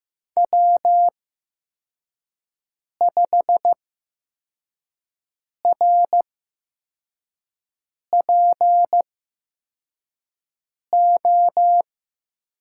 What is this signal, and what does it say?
Morse code, heard as a keyed tone, about 15 wpm, spaced as for 4 wpm, 700 Hz. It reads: W5RPO
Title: USA Call Signs - 5 Characters - 8x Character Spacing - 15wpm